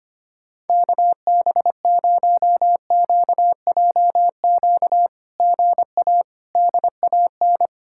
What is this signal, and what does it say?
Morse code, heard as a keyed tone, about 25 wpm, 700 Hz. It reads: KB0QJQ GA DAN
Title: SOTA QSO Practice 25wpm